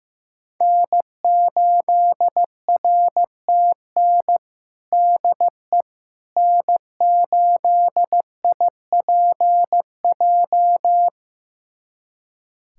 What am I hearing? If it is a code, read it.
Morse code, heard as a keyed tone, about 15 wpm, 700 Hz. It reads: N8RTN DE N8IPJ
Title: USA Callsign DE Callsign - 5 Characters - 15wpm